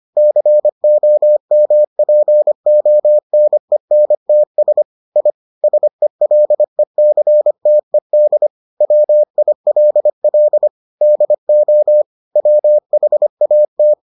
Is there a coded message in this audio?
Yes: Morse code from 0.2 to 13.9 s, heard as a keyed tone, about 25 wpm, 600 Hz. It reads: COMPONENTS I SELECTED WILL DO WHAT